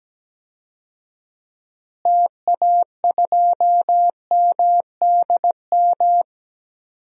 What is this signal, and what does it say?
Morse code, heard as a keyed tone, about 17 wpm, 700 Hz. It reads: TA2MDM
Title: DXCC-100 Call Signs - 6 Characters - Repeated 2x - 17wpm